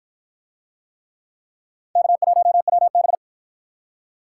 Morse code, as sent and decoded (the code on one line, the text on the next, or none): -.- .---- .-- -...
K1WB